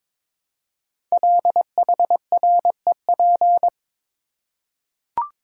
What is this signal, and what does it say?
Morse code, heard as a keyed tone, about 22 wpm, 700 Hz. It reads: LHREP
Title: Code Groups - 5 Letters - 22wpm